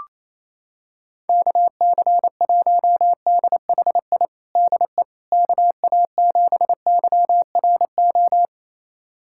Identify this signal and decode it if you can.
Morse code, heard as a keyed tone, about 28 wpm, 700 Hz. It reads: KC1DHI DE KA7YRO
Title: USA Callsign DE Callsign - 6 Characters - 28wpm